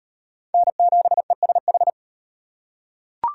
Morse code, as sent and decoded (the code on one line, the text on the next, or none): -. --... ... ....
N7ESH